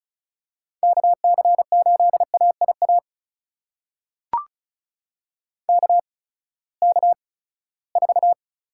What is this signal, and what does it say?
Morse code, heard as a keyed tone, about 36 wpm, spaced as for 20 wpm, 700 Hz. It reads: KC8AIA KK4
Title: USA Call Signs - 6 Characters - 8x Character Spacing - 35wpm